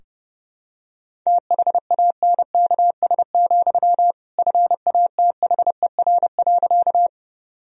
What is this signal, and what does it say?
Morse code, heard as a keyed tone, about 32 wpm, 700 Hz. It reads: THANKS, FATHER.
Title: Sentences from Top 1000 Words - Review Part 3 - 30wpm